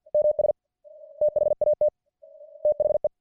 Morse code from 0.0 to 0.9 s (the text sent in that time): MS